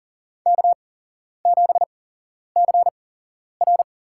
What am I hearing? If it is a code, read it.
Morse code, heard as a keyed tone, about 37 wpm, spaced as for 11 wpm, 700 Hz. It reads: K7CR
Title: USA Call Signs - 4 Characters - 8x Character Spacing - 40wpm